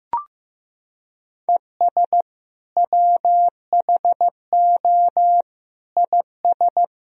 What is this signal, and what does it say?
Morse code, heard as a keyed tone, about 15 wpm, 700 Hz. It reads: ES WHO IS
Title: Sentences from Top 100 Words and Abbreviations 15wpm